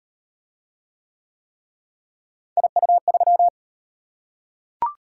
Morse code, heard as a keyed tone, about 38 wpm, 700 Hz. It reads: IU3